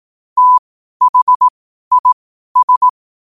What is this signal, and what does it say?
Morse code, heard as a keyed tone, about 18 wpm, spaced as for 12 wpm, 1000 Hz. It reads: THIS
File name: qotdfiles / 12wpm